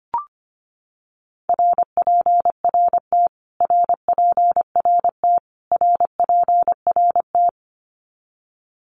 Morse code, heard as a keyed tone, about 25 wpm, 700 Hz. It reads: RPRT RPRT RPRT
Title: Abbreviations Repeated 3x - 25wpm